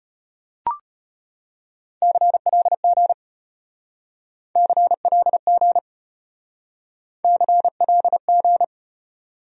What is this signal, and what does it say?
Morse code, heard as a keyed tone, about 34 wpm, 700 Hz. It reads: CLG CLG CLG